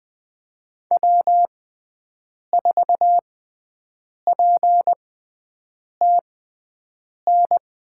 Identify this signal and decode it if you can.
Morse code, heard as a keyed tone, about 21 wpm, spaced as for 7 wpm, 700 Hz. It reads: W4PTN
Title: USA Call Signs - 6x Character Spacing - 20wpm